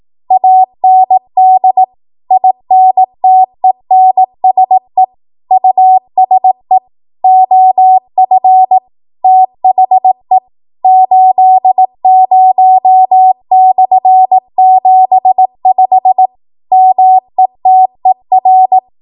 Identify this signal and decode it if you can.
Morse code, heard as a keyed tone, about 18 wpm, 750 Hz. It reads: AND INTENSE USE OF THE 80/75 METER
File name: Morse / 18WPM